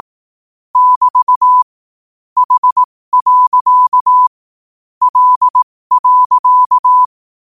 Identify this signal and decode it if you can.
Morse code, heard as a keyed tone, about 18 wpm, spaced as for 15 wpm, 1000 Hz. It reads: <BT> H. L.
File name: qotdfiles / 15wpm